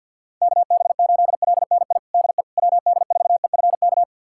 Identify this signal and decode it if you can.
Morse code, heard as a keyed tone, about 50 wpm, 700 Hz. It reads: KB8LNI DE WD4EFK
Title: USA Callsign DE Callsign 50wpm